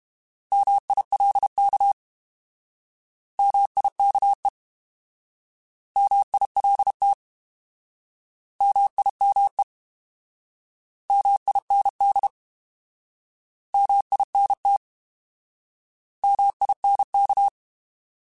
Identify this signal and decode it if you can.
Morse code, heard as a keyed tone, about 32 wpm, 800 Hz. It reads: MILK MIKE MILT MIME MIND MINT MINK
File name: Alphabetical / Alphabetical mp3s / Ms @30